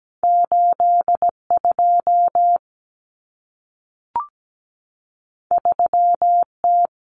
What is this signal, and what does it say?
Morse code, heard as a keyed tone, about 17 wpm, 700 Hz. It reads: 82 3T
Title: Common Cut Numbers - 2-Digit 17wpm